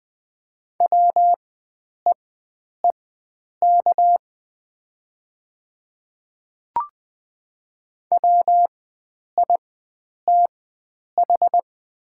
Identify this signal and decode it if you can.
Morse code, heard as a keyed tone, about 20 wpm, spaced as for 9 wpm, 700 Hz. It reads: WEEK WITH
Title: 4x Character Spacing - Top 500 Words - 20wpm